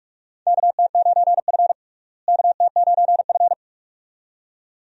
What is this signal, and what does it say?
Morse code, heard as a keyed tone, about 46 wpm, 700 Hz. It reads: KT9F KT9F
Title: USA Call Signs - 4 Characters - Repeated 2x - 45wpm